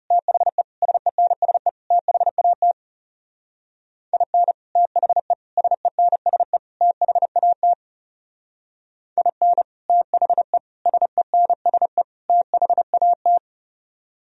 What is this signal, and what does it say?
Morse code, heard as a keyed tone, about 37 wpm, 700 Hz. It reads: THE SENSE THAT IN THE SENSE THAT IN THE SENSE THAT
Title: Speed-Racing - Sets of 4 Words 30wpm